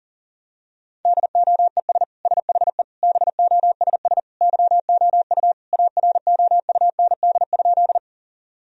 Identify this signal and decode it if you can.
Morse code, heard as a keyed tone, about 40 wpm, 700 Hz. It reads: DOES SHE BOSS YOU AROUND?
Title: Sentences from Top 1000 Words - Review Part 4 - 40wpm